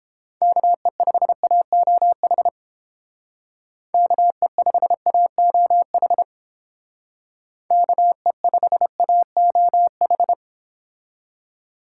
Morse code, heard as a keyed tone, about 30 wpm, 700 Hz. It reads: KE5AOH KE5AOH KE5AOH